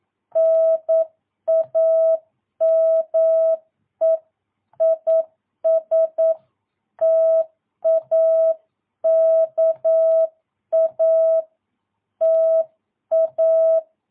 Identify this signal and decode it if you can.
Morse code, heard as a keyed tone, about 9 wpm, 650 Hz. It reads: NAME IS TAKA TA
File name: jh8cbh / 2022cw